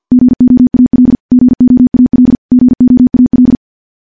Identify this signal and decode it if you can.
Morse code, heard as a keyed tone, about 49 wpm, 250 Hz. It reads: GOAL GOAL GOAL
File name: morse / morse-training-50WPM